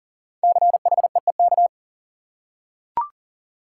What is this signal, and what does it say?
Morse code, heard as a keyed tone, about 40 wpm, 700 Hz. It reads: CHEEK
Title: Five Letter Words - 40wpm